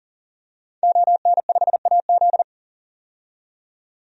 Morse code, heard as a keyed tone, about 40 wpm, 700 Hz. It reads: ON5AZ
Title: DXCC-100 Call Signs - 5 Characters - 40wpm